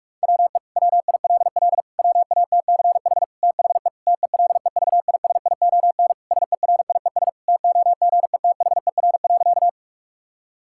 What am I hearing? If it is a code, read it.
Morse code, heard as a keyed tone, about 46 wpm, 700 Hz. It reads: WE WILL WATCH THE TELEVISION SERIES TOGETHER.
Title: Sentences from Top 1000 Words - Review Part 5 - 45wpm